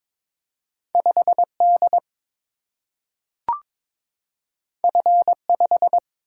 Morse code, heard as a keyed tone, about 22 wpm, 700 Hz. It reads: 5 D F 5
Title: Single-Letter Single-Number 22wpm